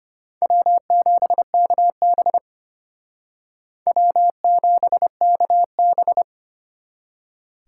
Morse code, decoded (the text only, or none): W7KB W7KB